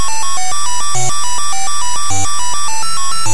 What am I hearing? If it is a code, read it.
There is no code.